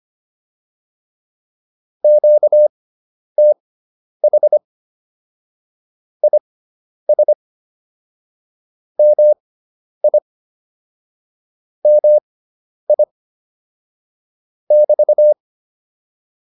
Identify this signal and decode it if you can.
Morse code, heard as a keyed tone, about 25 wpm, spaced as for 10 wpm, 600 Hz. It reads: QTH IS MI MI <BT>
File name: QSO108 10